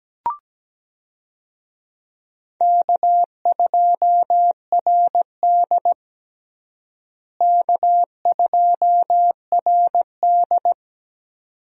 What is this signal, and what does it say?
Morse code, heard as a keyed tone, about 17 wpm, 700 Hz. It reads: K2RD K2RD
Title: USA Call Signs - 4 Characters - Repeated 2x - 17wpm